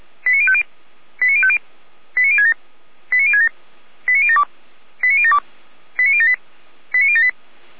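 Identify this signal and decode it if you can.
Five-tone selective call (ZVEI), heard as quick runs of five tones, tones of about 70 ms. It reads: 89050 89050 89067 89067 89072 89072 89078 89078
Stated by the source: Five-tone ZVEI selective call received by SDR.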